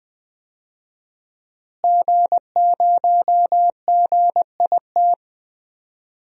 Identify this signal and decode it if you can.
Morse code, heard as a keyed tone, about 20 wpm, 700 Hz. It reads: G0GIT
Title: DXCC-100 Call Signs - 5 Characters - Repeated 2x - 20wpm